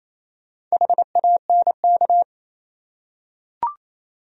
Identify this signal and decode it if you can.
Morse code, heard as a keyed tone, about 28 wpm, 700 Hz. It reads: HANK
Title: USA Common Names - Max 5 Letters - 28wpm